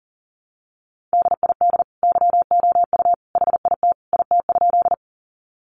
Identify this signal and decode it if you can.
Morse code, heard as a keyed tone, about 40 wpm, 700 Hz. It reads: DID YOU HIT IT?
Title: Sentences from Top 1200 Words 40wpm